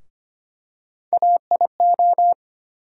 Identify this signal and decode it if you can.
Morse code, heard as a keyed tone, about 25 wpm, 700 Hz. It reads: AIO
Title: USA Call Sign Postfixes - 3 Letters 25wpm